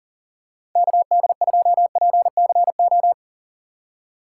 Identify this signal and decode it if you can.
Morse code, heard as a keyed tone, about 40 wpm, 700 Hz. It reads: KD2PCO
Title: USA Call Signs - 6 Characters - 2x Character Spacing - 40wpm